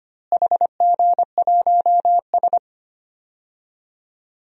Morse code, heard as a keyed tone, about 25 wpm, 700 Hz. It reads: HG1S
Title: DXCC-100 Call Signs - 4 Characters - Repeated 2x - 25wpm